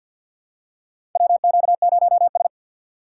Morse code, heard as a keyed tone, about 50 wpm, 690 Hz. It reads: WQ0S
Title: USA Call Signs - 4 Characters - 6x Character Spacing - 50wpm